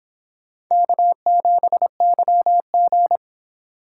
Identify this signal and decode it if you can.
Morse code, heard as a keyed tone, about 26 wpm, 700 Hz. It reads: K7YG